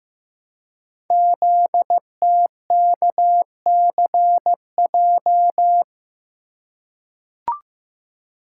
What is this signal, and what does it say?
Morse code, heard as a keyed tone, about 15 wpm, 700 Hz. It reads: ZTKCJ
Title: Code Groups - 5 Letters - 15wpm